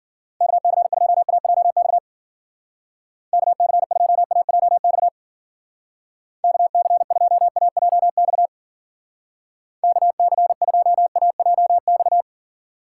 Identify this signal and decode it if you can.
Morse code, heard as a keyed tone, about 49 wpm, 700 Hz. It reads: KC2AJX KC2AJX KC2AJX KC2AJX
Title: Speed Racing - USA 2x3 Call Signs 40wpm